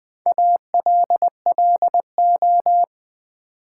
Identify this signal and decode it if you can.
Morse code, heard as a keyed tone, about 20 wpm, 700 Hz. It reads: ALLO
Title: Build A Word - 4 Letters At A Time - 10-Letter Words - 20wpm